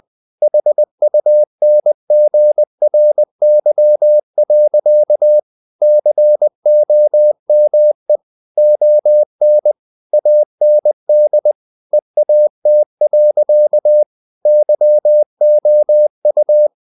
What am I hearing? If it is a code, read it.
Morse code, heard as a keyed tone, about 20 wpm, 600 Hz. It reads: HUNGRY. COME ON AND EAT. YOU
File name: the old man and the sea / the old man and the sea part 1 20wpm